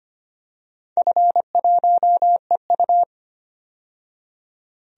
Morse code, heard as a keyed tone, about 25 wpm, 700 Hz. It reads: F1EU
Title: DXCC-100 Call Signs - 4 Characters - 25wpm